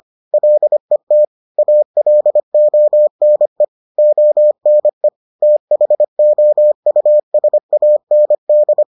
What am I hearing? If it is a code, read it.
Morse code, heard as a keyed tone, about 25 wpm, 600 Hz. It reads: LET ALONE ONE THOUSAND